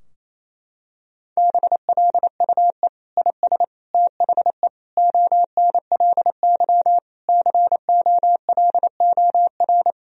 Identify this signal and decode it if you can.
Morse code, heard as a keyed tone, about 28 wpm, 700 Hz. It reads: BLUE IS THE ONLY COLOR
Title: Sentences from Top 1000 Words - Review Part 3 - 28wpm